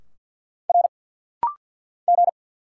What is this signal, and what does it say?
Morse code, heard as a keyed tone, about 50 wpm, 700 Hz. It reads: R G
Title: Single Letters - Rapid-Fire 50wpm